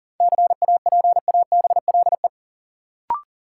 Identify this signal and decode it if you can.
Morse code, heard as a keyed tone, about 40 wpm, 700 Hz. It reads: CAPABLE
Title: Sentences from Top 1200 Words - New Word Drill Alt - 40wpm